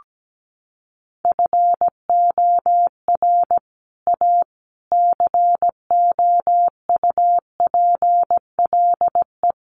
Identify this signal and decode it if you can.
Morse code, heard as a keyed tone, about 17 wpm, 700 Hz. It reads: FOR A COUPLE
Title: Sets of 5 Words Encore 17wpm